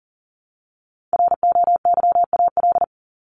Morse code, heard as a keyed tone, about 40 wpm, 700 Hz. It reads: ROYAL
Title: Top 1751-2000 Words 40wpm